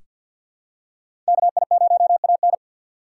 Morse code, heard as a keyed tone, about 47 wpm, 690 Hz. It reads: KI0AN
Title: USA Call Signs - 5 Characters - 4x Character Spacing - 50wpm